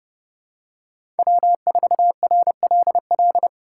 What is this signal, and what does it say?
Morse code, heard as a keyed tone, about 29 wpm, 700 Hz. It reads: W4RLL